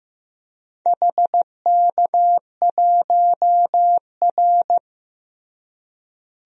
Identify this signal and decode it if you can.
Morse code, heard as a keyed tone, about 15 wpm, 700 Hz. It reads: HK1R
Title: DXCC-100 Call Signs - 4 Characters - Repeated 3x - 15wpm